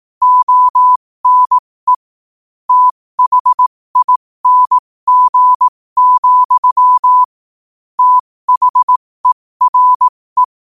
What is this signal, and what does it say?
Morse code, heard as a keyed tone, about 18 wpm, spaced as for 15 wpm, 1000 Hz. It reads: ONE THING, THERE
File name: qotdfiles / 15wpm